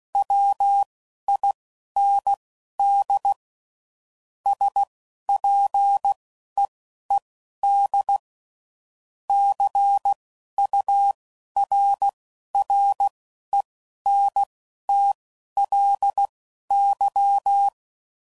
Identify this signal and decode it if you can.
Morse code, heard as a keyed tone, about 16 wpm, spaced as for 11 wpm, 800 Hz. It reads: WIND SPEED CURRENTLY
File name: CW Code Groups @10WPM / Weather Sets / Weather 5 22